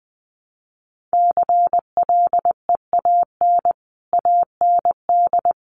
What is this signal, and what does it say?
Morse code, heard as a keyed tone, about 20 wpm, 700 Hz. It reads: CLEAN AND
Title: Sentences from Top 1100 Words 20wpm